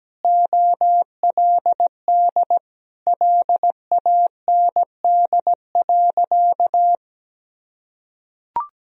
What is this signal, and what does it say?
Morse code, heard as a keyed tone, about 17 wpm, 700 Hz. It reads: OLD LAND.
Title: Sentences from Top 1000 Words - Review Part 1 - 17wpm